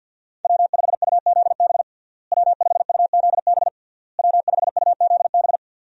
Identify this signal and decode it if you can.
Morse code, heard as a keyed tone, about 52 wpm, 700 Hz. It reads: W5UZB W5UZB W5UZB
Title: USA Call Signs - 5 Characters - Repeated 3x - 50wpm